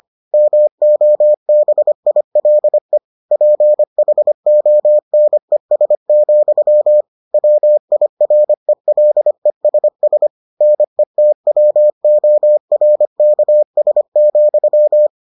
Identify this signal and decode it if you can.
Morse code, heard as a keyed tone, about 25 wpm, 600 Hz. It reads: MOBILE PHONES, WIRELESS NETWORKS,